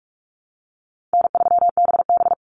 Morse code, heard as a keyed tone, about 44 wpm, 700 Hz. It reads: N3BB
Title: Easy Call Signs Repeated 2x - 45wpm